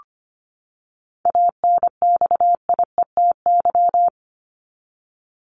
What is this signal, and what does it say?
Morse code, heard as a keyed tone, about 25 wpm, 700 Hz. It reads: ANXIETY
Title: Top 1501-1750 Words 25wpm